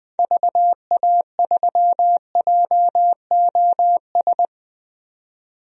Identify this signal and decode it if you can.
Morse code, heard as a keyed tone, about 20 wpm, 700 Hz. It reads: VA3JOS
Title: DXCC-100 Call Signs - 6 Characters - 20wpm